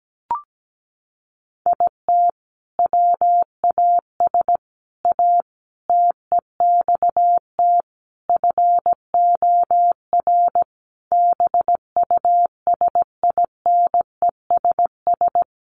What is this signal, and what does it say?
Morse code, heard as a keyed tone, about 17 wpm, 700 Hz. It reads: IT WAS A TEXT FOR BUSINESS
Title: Sentences from Top 1200 Words 17wpm